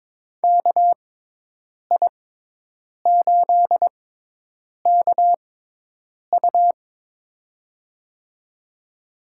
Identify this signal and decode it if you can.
Morse code, heard as a keyed tone, about 22 wpm, spaced as for 8 wpm, 700 Hz. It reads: KI8KU
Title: USA Call Signs - 5 Characters - 6x Character Spacing - 22wpm